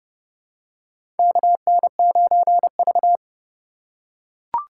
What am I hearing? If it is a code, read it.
Morse code, heard as a keyed tone, about 32 wpm, 700 Hz. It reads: KN9V
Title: USA Call Signs - 4 Characters - 4x Character Spacing - 30wpm